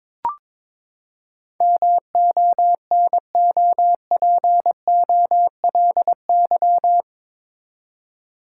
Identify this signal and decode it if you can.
Morse code, heard as a keyed tone, about 22 wpm, 700 Hz. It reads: MONOPOLY
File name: practice / 22wpm / Eight Letter Words - 22wpm